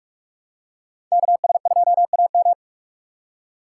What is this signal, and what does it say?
Morse code, heard as a keyed tone, about 44 wpm, 700 Hz. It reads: KS2AM